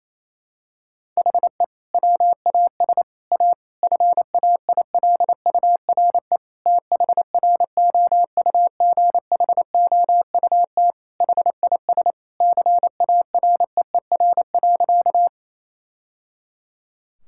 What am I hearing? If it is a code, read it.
Morse code, heard as a keyed tone, about 28 wpm, 700 Hz. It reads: HE WAS A FAILURE THROUGHOUT HIS CAREER.
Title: Sentences from Top 1000 Words - Review Part 6 - 28wpm